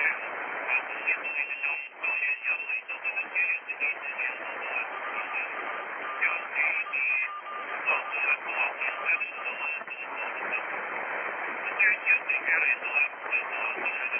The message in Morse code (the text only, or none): none